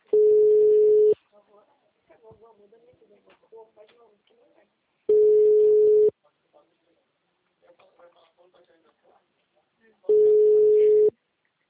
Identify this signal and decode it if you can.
Ringback tone (Europe, 425 Hz), with speech.